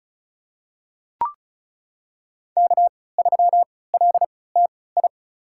Morse code, heard as a keyed tone, about 36 wpm, spaced as for 20 wpm, 700 Hz. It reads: K3LTI
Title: USA Call Signs - 5 Characters - 3x Character Spacing - 35wpm